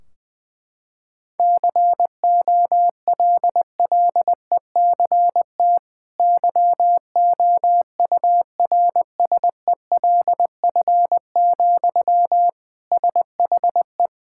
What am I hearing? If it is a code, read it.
Morse code, heard as a keyed tone, about 20 wpm, 700 Hz. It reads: COLLECT YOURSELF, SHE
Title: Sentences from Top 1600 Words 20wpm